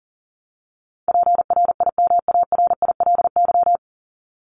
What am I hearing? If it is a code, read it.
Morse code, heard as a keyed tone, about 40 wpm, 700 Hz. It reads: PRIMARILY
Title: Top 1001-1250 Words 40wpm